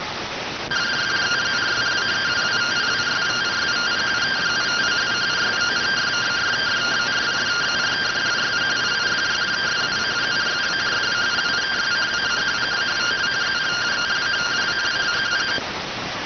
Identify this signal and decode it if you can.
Two-tone teleprinter signal, about 170 Hz shift.